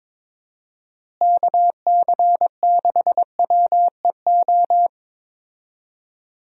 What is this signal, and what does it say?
Morse code, heard as a keyed tone, about 22 wpm, 700 Hz. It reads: KC6WEO
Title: USA Call Signs - 6 Characters - Repeated 2x - 22wpm